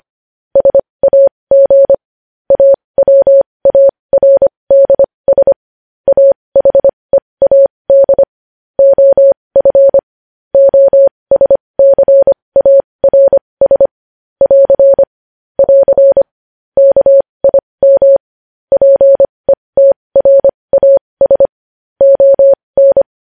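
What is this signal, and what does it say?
Morse code, heard as a keyed tone, about 25 wpm, spaced as for 20 wpm, 550 Hz. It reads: SAG AWARDS AHEAD OF OSCARS <AR> <AR> KIM PETRAS ON
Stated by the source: Morse code news headlines practice at 20 wpm.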